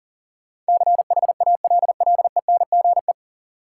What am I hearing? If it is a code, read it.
Morse code, heard as a keyed tone, about 37 wpm, 700 Hz. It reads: CHALLENGE
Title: Sentences from Top 1000 Words - Review Part 7 - New Word Drill - 40wpm